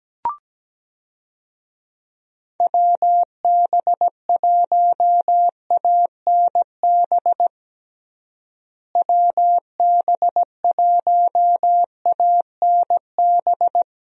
Morse code, heard as a keyed tone, about 17 wpm, 700 Hz. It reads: WB1ANB WB1ANB
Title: USA Call Signs - 6 Characters - Repeated 3x - 17wpm